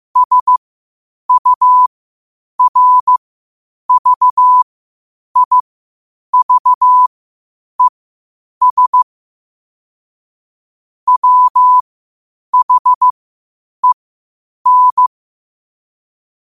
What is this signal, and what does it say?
Morse code, heard as a keyed tone, about 15 wpm, spaced as for 8 wpm, 1000 Hz. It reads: SURVIVES WHEN
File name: 8wpm